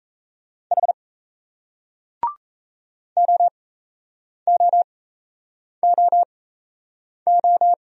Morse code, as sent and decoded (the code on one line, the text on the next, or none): .... / --- --- --- ---
H OOOO